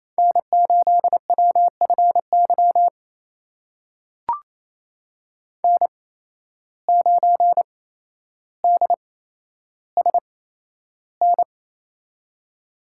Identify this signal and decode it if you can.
Morse code, heard as a keyed tone, about 27 wpm, spaced as for 12 wpm, 700 Hz. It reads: N8WFY N9DSN
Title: USA Call Signs - 5 Characters - 8x Character Spacing - 28wpm